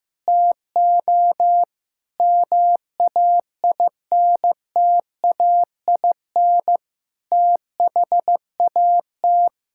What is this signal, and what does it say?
Morse code, heard as a keyed tone, about 15 wpm, 700 Hz. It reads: TO MAINTAIN THAT
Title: Sentences from Top 1000 Words - Review Part 5 - 15wpm